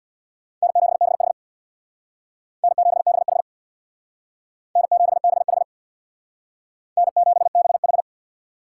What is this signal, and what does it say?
Morse code, heard as a keyed tone, about 61 wpm, 700 Hz. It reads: N7BH N7BH N7BH N7BH
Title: Speed Racing - USA 1x2 Call Signs 50wpm